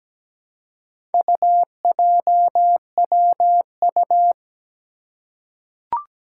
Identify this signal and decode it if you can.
Morse code, heard as a keyed tone, about 17 wpm, 700 Hz. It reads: UJWU
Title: Four Challenging Letter-Number Combinations 17wpm